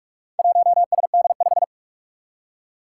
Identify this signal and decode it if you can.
Morse code, heard as a keyed tone, about 44 wpm, 700 Hz. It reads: 1SD5
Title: Four Challenging Letter-Number Combinations 45wpm